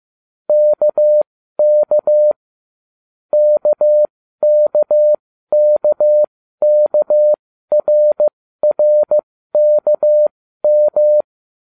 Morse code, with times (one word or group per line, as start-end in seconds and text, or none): 0.5-2.3 KK
3.3-11.2 KKKKRRKM